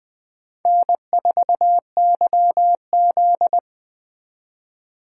Morse code, heard as a keyed tone, about 21 wpm, 700 Hz. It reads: N4YZ
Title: Speed Racing - USA 1x2 Call Signs 17wpm